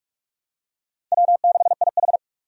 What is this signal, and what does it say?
Morse code, heard as a keyed tone, about 46 wpm, 700 Hz. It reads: W6IH
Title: USA Call Signs - 4 Characters - 45wpm